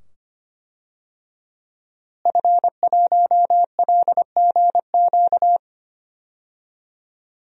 Morse code, heard as a keyed tone, about 25 wpm, 700 Hz. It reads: F1LGQ
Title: DXCC-100 Call Signs - 5 Characters - Repeated 2x - 25wpm